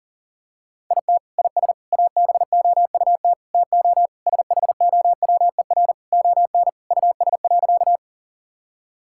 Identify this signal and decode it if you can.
Morse code, heard as a keyed tone, about 40 wpm, 700 Hz. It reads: IT IS ABOUT TO SHOWER ON US.
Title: Sentences from Top 1600 Words 40wpm